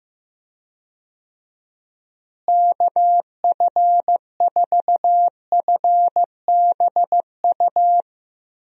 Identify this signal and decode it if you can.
Morse code, heard as a keyed tone, about 15 wpm, 700 Hz. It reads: KF4FBU
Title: USA Call Signs - 6 Characters - Repeated 2x - 15wpm